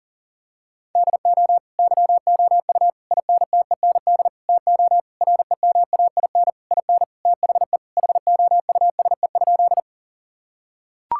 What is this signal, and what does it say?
Morse code, heard as a keyed tone, about 39 wpm, 700 Hz. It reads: DO YOU INTEND TO REMAIN IN THE HOUSE?
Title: Sentences from Top 1600 Words 40wpm